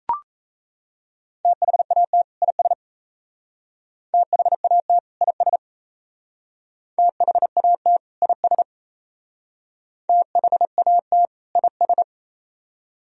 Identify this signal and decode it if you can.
Morse code, heard as a keyed tone, about 34 wpm, 700 Hz. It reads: THAT IS THAT IS THAT IS THAT IS
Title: Speed-Racing - Sets of 2 Words 28wpm